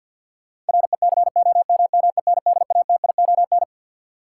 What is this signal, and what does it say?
Morse code, heard as a keyed tone, about 47 wpm, 700 Hz. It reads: RECOMMENDATION
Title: Sentences from Top 1400 Words - New Word Drill - 50wpm